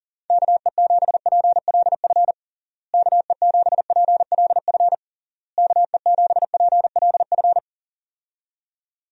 Morse code, heard as a keyed tone, about 40 wpm, 700 Hz. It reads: KE7PLF KE7PLF KE7PLF
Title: USA Call Signs - 6 Characters - Repeated 3x - 40wpm